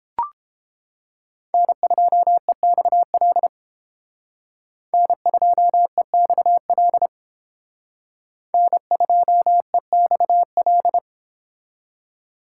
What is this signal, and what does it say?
Morse code, heard as a keyed tone, about 31 wpm, 700 Hz. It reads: N2EXL N2EXL N2EXL